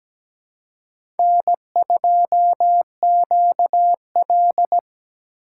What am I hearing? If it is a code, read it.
Morse code, heard as a keyed tone, about 17 wpm, 700 Hz. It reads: N2QL
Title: USA Call Signs - 4x Character Spacing - 17wpm